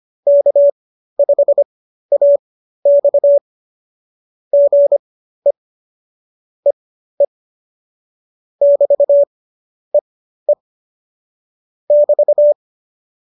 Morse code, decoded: K5AX GE EE <BT> EE <BT>

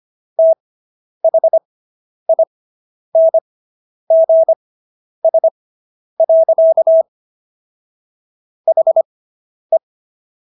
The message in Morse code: THINGS. HE

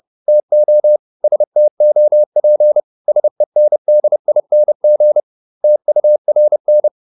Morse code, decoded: TO STOP SENDING TURN

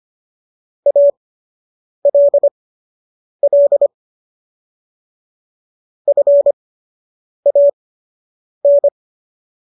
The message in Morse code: ALL FAN

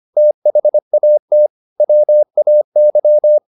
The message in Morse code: THAT WAY

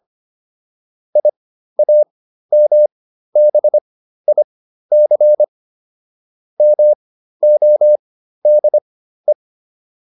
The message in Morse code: IAMBIC MODE

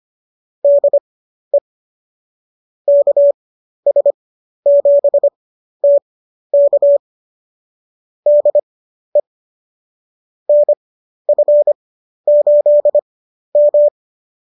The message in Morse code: DE KS7T K DE NF8M

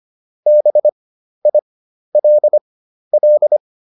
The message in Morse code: BILL

